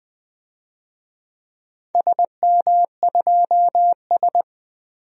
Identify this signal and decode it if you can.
Morse code, heard as a keyed tone, about 20 wpm, 700 Hz. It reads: SM2S
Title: DXCC-100 Call Signs - 4 Characters - 20wpm